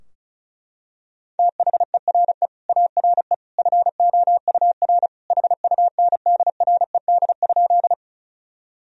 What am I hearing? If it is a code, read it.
Morse code, heard as a keyed tone, about 36 wpm, 700 Hz. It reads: THERE ARE FOUR HUNDRED?
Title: Sentences from Top 1000 Words - Review Part 2 - 35wpm